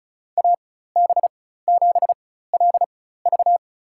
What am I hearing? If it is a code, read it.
Morse code, heard as a keyed tone, about 36 wpm, spaced as for 16 wpm, 700 Hz. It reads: AB7LV
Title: USA Call Signs - 5 Characters - 4x Character Spacing - 35wpm